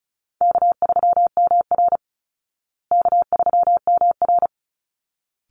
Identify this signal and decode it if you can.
Morse code, heard as a keyed tone, about 34 wpm, 700 Hz. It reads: K3MR K3MR
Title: Call Signs Repeated 3x - 35wpm